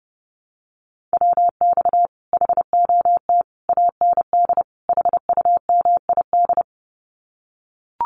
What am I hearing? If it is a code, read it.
Morse code, heard as a keyed tone, about 30 wpm, 700 Hz. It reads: WX HOT AND HUMID
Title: Common QSO Phrases Encore 30wpm